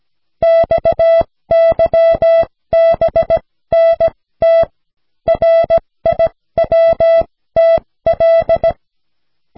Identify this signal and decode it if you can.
Morse code, heard as a keyed tone, about 17 wpm, spaced as for 14 wpm, 660 Hz. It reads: XYBNT RIWTL
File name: cw / snelheid 4